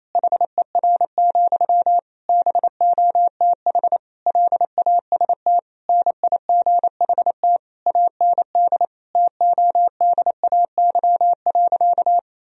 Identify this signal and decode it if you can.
Morse code, heard as a keyed tone, about 28 wpm, 700 Hz. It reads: HER, BOTH LAST NIGHT AND TODAY.
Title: Sentences from Top 1000 Words - Review Part 4 - 28wpm